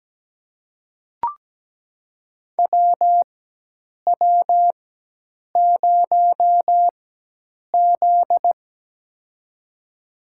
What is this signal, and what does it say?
Morse code, heard as a keyed tone, about 17 wpm, spaced as for 8 wpm, 700 Hz. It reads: WW0Z
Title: USA Call Signs - 4 Characters - 4x Character Spacing - 17wpm